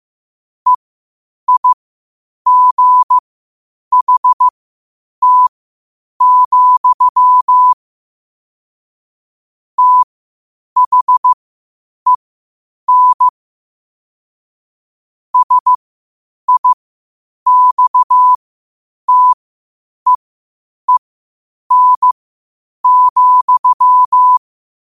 Morse code, heard as a keyed tone, about 15 wpm, spaced as for 8 wpm, 1000 Hz. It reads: EIGHT, THEN SIXTEEN,
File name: 8wpm